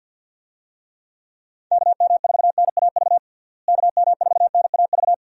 Morse code, heard as a keyed tone, about 51 wpm, 700 Hz. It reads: KM4NAV KM4NAV
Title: USA Call Signs - 6 Characters - Repeated 3x - 50wpm